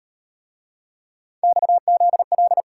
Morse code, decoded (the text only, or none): XZL